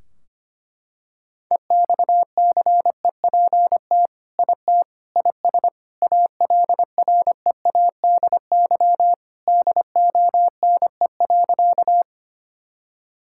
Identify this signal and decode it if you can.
Morse code, heard as a keyed tone, about 25 wpm, 700 Hz. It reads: EXCEPT IT IS ALREADY DONE.
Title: Sentences from Top 1600 Words 25wpm